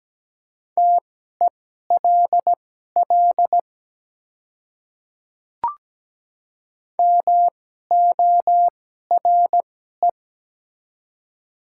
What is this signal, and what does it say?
Morse code, heard as a keyed tone, about 17 wpm, spaced as for 12 wpm, 700 Hz. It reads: TELL MORE
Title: 2x Character Spacing - Top 200 Words - 17wpm